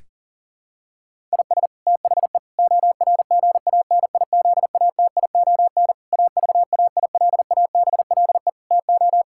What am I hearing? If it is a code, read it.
Morse code, heard as a keyed tone, about 40 wpm, 700 Hz. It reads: IS THE ORGANIZATION AVAILABLE TO
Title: Sentences from Top 1000 Words - Review Part 5 - 40wpm